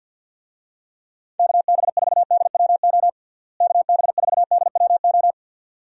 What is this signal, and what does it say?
Morse code, heard as a keyed tone, about 47 wpm, 700 Hz. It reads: KB4DWO KB4DWO
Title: USA Call Signs - 6 Characters - Repeated 2x - 50wpm